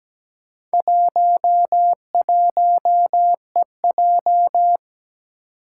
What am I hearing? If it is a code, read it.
Morse code, heard as a keyed tone, about 17 wpm, 700 Hz. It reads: 11EJ
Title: Four Challenging Letter-Number Combinations 17wpm